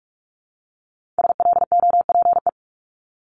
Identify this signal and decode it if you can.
Morse code, heard as a keyed tone, about 46 wpm, 700 Hz. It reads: SLOPE
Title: Multi-letter Phonemes Repeated 3x 45wpm